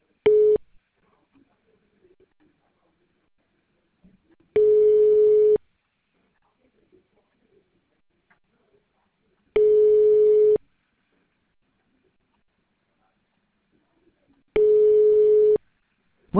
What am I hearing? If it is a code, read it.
Ringback tone (Europe, 425 Hz).